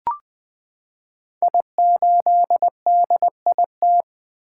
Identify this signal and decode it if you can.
Morse code, heard as a keyed tone, about 20 wpm, 700 Hz. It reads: I8DIT
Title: Code Groups - 5 Letter:Numbers - 20wpm